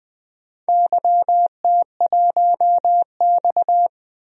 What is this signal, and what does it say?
Morse code, heard as a keyed tone, about 21 wpm, 700 Hz. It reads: YT1X
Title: DXCC-100 Call Signs - 4 Characters - Repeated 3x - 20wpm